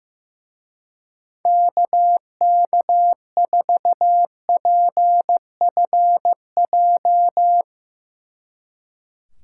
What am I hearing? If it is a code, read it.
Morse code, heard as a keyed tone, about 15 wpm, 700 Hz. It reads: KK4PFJ
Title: Speed Racing - USA 2x3 Call Signs 15wpm